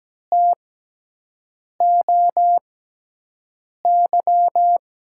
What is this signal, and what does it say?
Morse code, heard as a keyed tone, about 17 wpm, spaced as for 6 wpm, 700 Hz. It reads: TOY
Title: USA Call Signs - 6 Characters - 6x Character Spacing - 17wpm